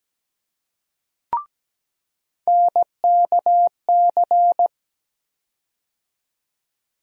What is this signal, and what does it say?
Morse code, heard as a keyed tone, about 17 wpm, 700 Hz. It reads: NKC
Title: USA Call Sign Postfixes - 3 Letters 17wpm